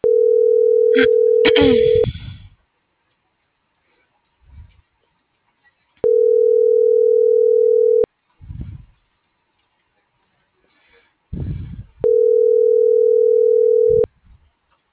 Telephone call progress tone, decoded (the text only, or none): ringback tone